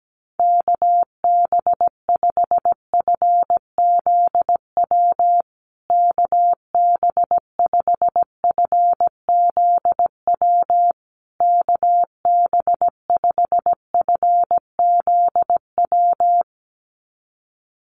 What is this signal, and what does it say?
Morse code, heard as a keyed tone, about 17 wpm, 700 Hz. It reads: KB5FZW KB5FZW KB5FZW
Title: USA Call Signs Repeated 3x 17wpm